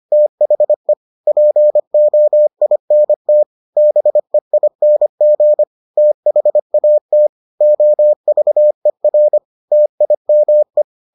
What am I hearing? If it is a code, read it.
Morse code, heard as a keyed tone, about 25 wpm, 600 Hz. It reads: THE POINT BEING THAT OVER TIME